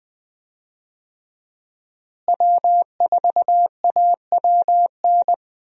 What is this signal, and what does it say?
Morse code, heard as a keyed tone, about 21 wpm, 700 Hz. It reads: W4AWN
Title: USA Call Signs - 5 Characters - Repeated 3x - 20wpm